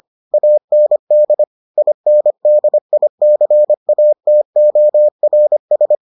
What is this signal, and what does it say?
Morse code, heard as a keyed tone, about 25 wpm, 600 Hz. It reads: AND INDICATORS